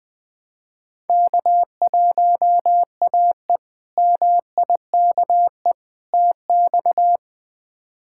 Morse code, heard as a keyed tone, about 20 wpm, 700 Hz. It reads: K1AE MIKE TX
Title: K1USN SST QSO Practice - 20wpm